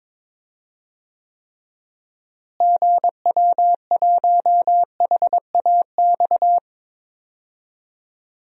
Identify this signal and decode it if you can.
Morse code, heard as a keyed tone, about 22 wpm, 700 Hz. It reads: GW1HAX
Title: DXCC-100 Call Signs - 6 Characters - Repeated 3x - 22wpm